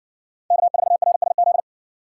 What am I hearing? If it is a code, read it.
Morse code, heard as a keyed tone, about 62 wpm, 700 Hz. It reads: K4RSZ